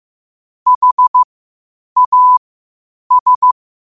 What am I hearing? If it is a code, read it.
Morse code, heard as a keyed tone, about 15 wpm, spaced as for 8 wpm, 1000 Hz. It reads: HAS
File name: qotdfiles / 8wpm